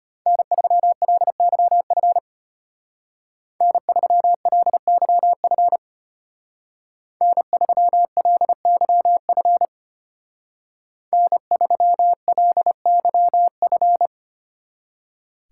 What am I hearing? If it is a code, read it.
Morse code, heard as a keyed tone, about 30 wpm, 700 Hz. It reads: N3LYF N3LYF N3LYF N3LYF
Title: Speed Racing - USA 1x3 Call Signs 25wpm